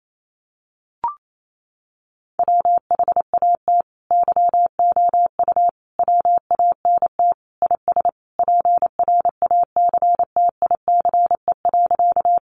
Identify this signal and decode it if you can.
Morse code, heard as a keyed tone, about 28 wpm, 700 Hz. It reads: WHAT YOU WANT IS PRACTICE.
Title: Sentences from Top 700 Words 28wpm